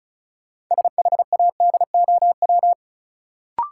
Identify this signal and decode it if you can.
Morse code, heard as a keyed tone, about 34 wpm, 700 Hz. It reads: SHADOW
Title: Sets of 5 Words Encore - New Word Drill - 35wpm